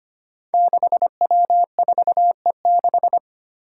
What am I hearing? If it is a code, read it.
Morse code, heard as a keyed tone, about 25 wpm, 700 Hz. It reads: 6W4E6
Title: Five Challenging Letter-Number Combinations 25wpm